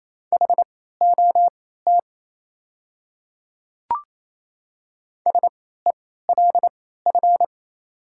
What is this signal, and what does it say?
Morse code, heard as a keyed tone, about 28 wpm, spaced as for 16 wpm, 700 Hz. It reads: HOT SELF
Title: 3x Character Spacing - Top 200 Words - 28wpm